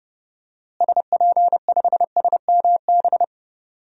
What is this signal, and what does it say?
Morse code, heard as a keyed tone, about 30 wpm, 700 Hz. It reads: SP5SMB
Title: DXCC-100 Call Signs - 6 Characters - Repeated 2x - 30wpm